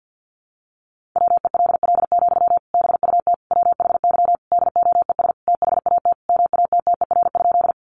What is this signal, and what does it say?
Morse code, heard as a keyed tone, about 50 wpm, 700 Hz. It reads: WELL, BUT WHY DOES THAT MATTER?
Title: Sentences from Top 700 Words 50wpm